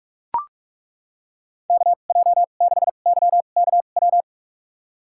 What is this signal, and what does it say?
Morse code, heard as a keyed tone, about 43 wpm, spaced as for 32 wpm, 690 Hz. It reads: KJ6YKW
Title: USA Call Signs - 6 Characters - 2x Character Spacing - 45wpm